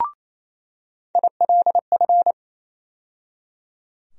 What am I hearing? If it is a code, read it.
Morse code, heard as a keyed tone, about 28 wpm, 700 Hz. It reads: ILF
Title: USA Call Sign Postfixes - 3 Letters 28wpm